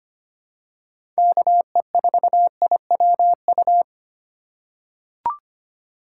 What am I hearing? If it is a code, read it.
Morse code, heard as a keyed tone, about 25 wpm, 700 Hz. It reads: KE4IWU
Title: USA Call Signs - 3x Character Spacing - 25wpm